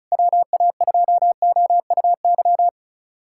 Morse code, heard as a keyed tone, about 35 wpm, 700 Hz. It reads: WA2OUY
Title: USA Callsign DE Callsign 35wpm